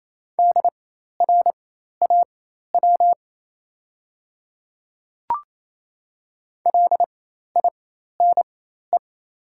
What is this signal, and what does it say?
Morse code, heard as a keyed tone, about 28 wpm, spaced as for 13 wpm, 700 Hz. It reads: DRAW LINE
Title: 4x Character Spacing - Top 300 Words - 28wpm